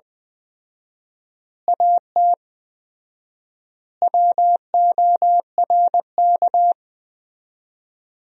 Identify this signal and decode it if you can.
Morse code, heard as a keyed tone, about 21 wpm, 700 Hz. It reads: AT WORK
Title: 4x Word-Spacing - Sentences from Top 200 Words - 20wpm